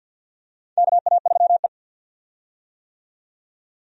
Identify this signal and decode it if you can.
Morse code, heard as a keyed tone, about 52 wpm, 700 Hz. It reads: KA3E
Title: USA Call Signs - 4 Characters - 50wpm